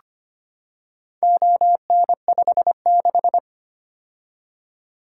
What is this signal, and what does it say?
Morse code, heard as a keyed tone, about 25 wpm, 700 Hz. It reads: ON56
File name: practice / 25wpm / Four Letter-Number Combinations - 25wpm